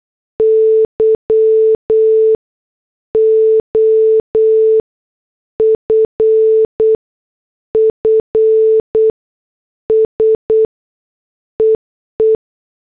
Morse code, heard as a keyed tone, about 8 wpm, spaced as for 6 wpm, 440 Hz. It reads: YOFFSEE